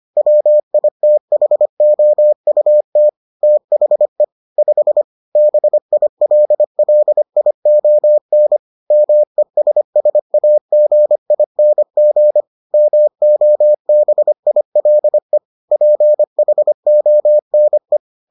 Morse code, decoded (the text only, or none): WITHOUT THE 5 BILLION MESSAGING MOBILE PHONE